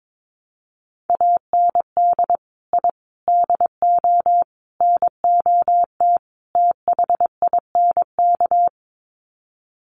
Morse code, heard as a keyed tone, about 22 wpm, 700 Hz. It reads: AND I DO NOT THINK
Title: Sets of 5 Words 22wpm